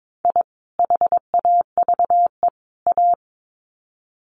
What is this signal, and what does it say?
Morse code, heard as a keyed tone, about 22 wpm, 700 Hz. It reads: I HAVE A
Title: Sets of 3 Words 22wpm